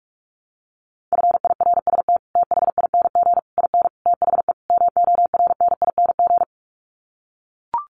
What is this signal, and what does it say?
Morse code, heard as a keyed tone, about 44 wpm, 700 Hz. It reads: FIRST THING IN THE MORNING
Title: Sets of 5 Words Encore2 45wpm